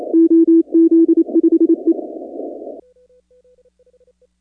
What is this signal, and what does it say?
Morse code, heard as a keyed tone, about 28 wpm, 330 Hz. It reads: OZ5E
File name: contests / 2019-cw-wpx-kt5j-k5pi- -2330 0-25